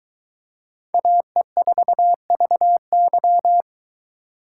Morse code, heard as a keyed tone, about 23 wpm, 700 Hz. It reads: AE4VY